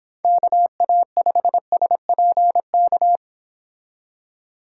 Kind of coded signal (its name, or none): Morse code